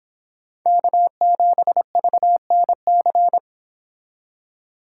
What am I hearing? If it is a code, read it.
Morse code, heard as a keyed tone, about 26 wpm, 700 Hz. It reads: K7VNC
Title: Speed Racing - USA 1x3 Call Signs 22wpm